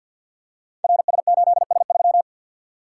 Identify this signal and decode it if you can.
Morse code, heard as a keyed tone, about 53 wpm, 700 Hz. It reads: RS8S3
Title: Code Groups - 5 Letter:Numbers - 50wpm